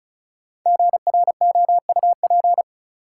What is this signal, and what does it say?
Morse code, heard as a keyed tone, about 36 wpm, 700 Hz. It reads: GROUP
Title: Sentences from Top 1000 Words - Review Part 1 - New Word Drill - 35wpm